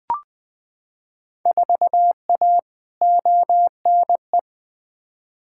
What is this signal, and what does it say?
Morse code, heard as a keyed tone, about 20 wpm, 700 Hz. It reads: 4A ONE
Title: Field Day Exchanges Encore - 20wpm